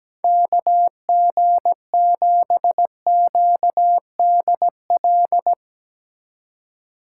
Morse code, heard as a keyed tone, about 17 wpm, 700 Hz. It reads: KG7QDL
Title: USA Call Signs - 6 Characters - 4x Character Spacing - 17wpm